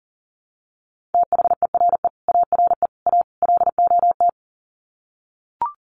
Morse code, heard as a keyed tone, about 40 wpm, 700 Hz. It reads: THERE ARE A LOT
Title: Sets of 4 Words 40wpm